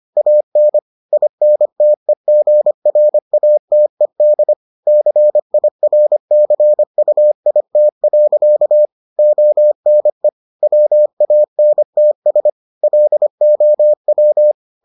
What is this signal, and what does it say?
Morse code, heard as a keyed tone, about 25 wpm, 600 Hz. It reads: AN INTEGRATED CIRCUIT. ONE WANTS LOW